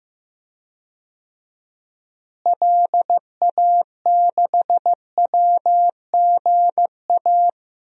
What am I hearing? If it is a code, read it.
Morse code, heard as a keyed tone, about 15 wpm, 700 Hz. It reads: LA6WGA
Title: DXCC-100 Call Signs - 6 Characters - Repeated 2x - 15wpm